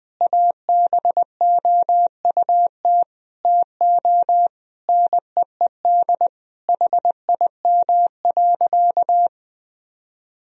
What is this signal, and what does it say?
Morse code, heard as a keyed tone, about 20 wpm, 700 Hz. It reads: ABOUT TO NEED HIM.
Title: Sentences from Top 1000 Words - Review Part 1 - 20wpm